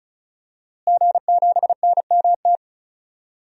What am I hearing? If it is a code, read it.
Morse code, heard as a keyed tone, about 36 wpm, 700 Hz. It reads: G7NMT
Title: DXCC-100 Call Signs - 5 Characters - Repeated 3x - 35wpm